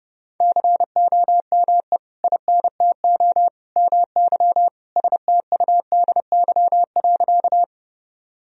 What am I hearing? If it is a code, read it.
Morse code, heard as a keyed tone, about 32 wpm, 700 Hz. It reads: COME INTO MY STUDY.
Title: Sentences from Top 1000 Words - Review Part 1 - 30wpm